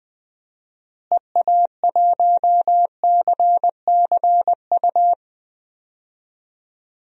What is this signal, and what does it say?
Morse code, heard as a keyed tone, about 20 wpm, 700 Hz. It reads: EA1CCU
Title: DXCC-100 Call Signs - 6 Characters - Repeated 3x - 20wpm